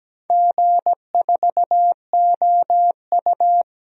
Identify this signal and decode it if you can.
Morse code, heard as a keyed tone, about 17 wpm, 700 Hz. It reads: G4OU
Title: DXCC-100 Call Signs - 4 Characters - Repeated 3x - 17wpm